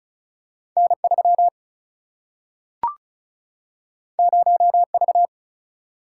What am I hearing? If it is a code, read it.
Morse code, heard as a keyed tone, about 36 wpm, 700 Hz. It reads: N 3 0 V
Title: Single-Letter Single-Number 35wpm